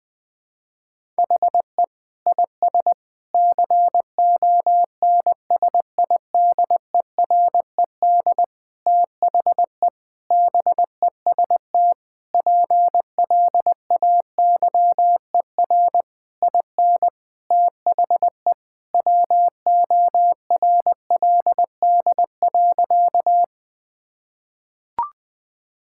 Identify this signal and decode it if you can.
Morse code, heard as a keyed tone, about 20 wpm, 700 Hz. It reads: HE IS CONSIDERED THE BEST PLAYER IN THE WORLD.
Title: Sentences with 6-Word Phrases - 20wpm